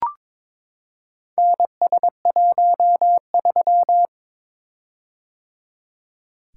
Morse code, heard as a keyed tone, about 22 wpm, 700 Hz. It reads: NS13
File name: practice / 22wpm / Four Letter-Number Combinations - 22wpm